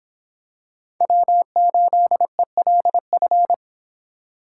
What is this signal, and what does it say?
Morse code, heard as a keyed tone, about 26 wpm, 700 Hz. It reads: W8ELF